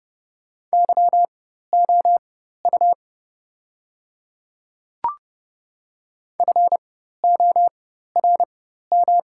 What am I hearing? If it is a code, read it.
Morse code, heard as a keyed tone, about 30 wpm, spaced as for 14 wpm, 690 Hz. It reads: YOU FORM
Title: 4x Character Spacing - Top 500 Words - 30wpm